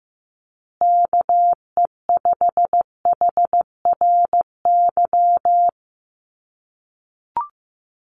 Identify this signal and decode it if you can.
Morse code, heard as a keyed tone, about 15 wpm, 700 Hz. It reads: KE5HRY
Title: USA Call Signs Repeated 2x 15wpm